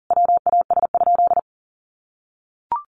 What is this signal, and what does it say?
Morse code, heard as a keyed tone, about 40 wpm, 700 Hz. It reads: WAS?
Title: Sentences from Top 700 Words 40wpm